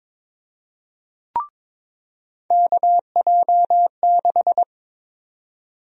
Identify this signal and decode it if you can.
Morse code, heard as a keyed tone, about 22 wpm, 700 Hz. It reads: KJ6
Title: USA Call Sign Prefixes - 22wpm